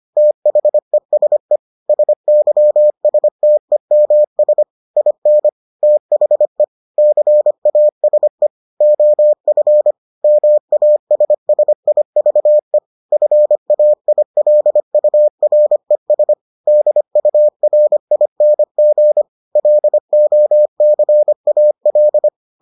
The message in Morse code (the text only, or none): THESE SYSTEMS IN THE CASE OF MASSIVE FAILURES DURING LOCAL